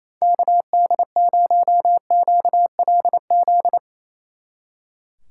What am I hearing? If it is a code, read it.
Morse code, heard as a keyed tone, about 34 wpm, 700 Hz. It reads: KD0QLZ
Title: Speed Racing - USA 2x3 Call Signs 28wpm